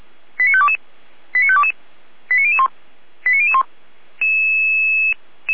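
Five-tone selective call (ZVEI), heard as quick runs of five tones, tones of about 70 ms. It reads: 89522 89522 89001 89001
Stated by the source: ZVEI five-tone selective call received on SDR.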